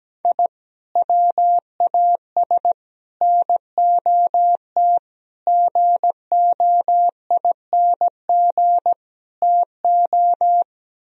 Morse code, heard as a keyed tone, about 17 wpm, 700 Hz. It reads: I WAS NOT GOING TO
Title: Speed-Racing - Sets of 5 Words 17wpm